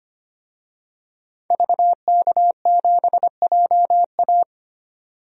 Morse code, heard as a keyed tone, about 25 wpm, 700 Hz. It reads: VK7JA
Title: DXCC-100 Call Signs - 5 Characters - 25wpm